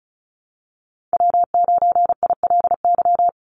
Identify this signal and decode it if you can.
Morse code, heard as a keyed tone, about 36 wpm, 700 Hz. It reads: W9ILY
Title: Call Signs - 35wpm